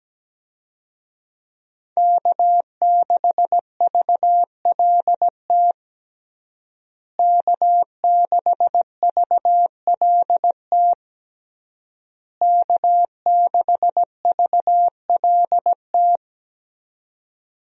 Morse code, heard as a keyed tone, about 17 wpm, 700 Hz. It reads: K6VLT K6VLT K6VLT
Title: USA Call Signs - 5 Characters - Repeated 3x - 17wpm